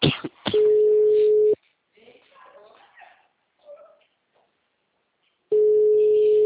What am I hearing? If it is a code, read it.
Ringback tone (Europe, 425 Hz).
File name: out- -2014- - - .22221